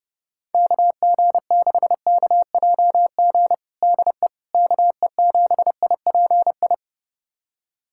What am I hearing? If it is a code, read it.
Morse code, heard as a keyed tone, about 30 wpm, 700 Hz. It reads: KG6KJG DE KE7IPI
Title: USA Callsign DE Callsign - 6 Characters - 30wpm